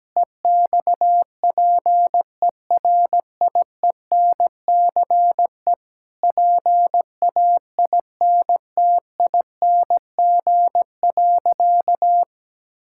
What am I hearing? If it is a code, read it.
Morse code, heard as a keyed tone, about 17 wpm, 700 Hz. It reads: EXPERIENCE PAINTING.
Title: Sentences from Top 1000 Words - Review Part 6 - 17wpm